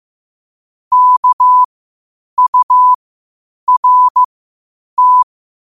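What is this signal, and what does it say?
Morse code, heard as a keyed tone, about 15 wpm, spaced as for 8 wpm, 1000 Hz. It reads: KURT
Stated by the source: Morse code quotes at 8 WPM.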